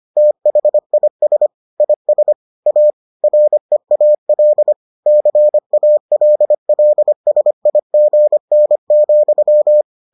Morse code, heard as a keyed tone, about 25 wpm, 600 Hz. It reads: THIS IS A REAL CALLSIGN,